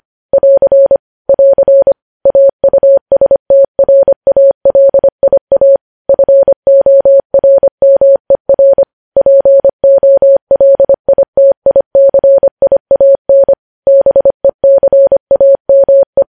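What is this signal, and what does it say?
Morse code, heard as a keyed tone, about 25 wpm, 550 Hz. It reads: <AR> <AR> AUSTRALIA FORMER POLITICIAN BECAME
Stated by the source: Morse news headlines sent at 25 wpm.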